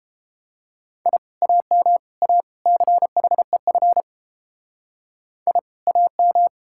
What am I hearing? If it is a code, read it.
Morse code, heard as a keyed tone, about 29 wpm, 700 Hz. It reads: I AM A CHEF I AM